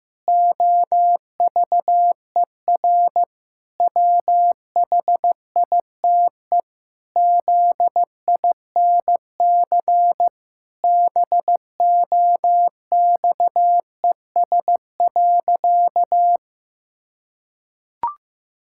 Morse code, heard as a keyed tone, about 15 wpm, 700 Hz. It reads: OVER WHITE ZINC BOXES.
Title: Pangram Challenge v2 - 15wpm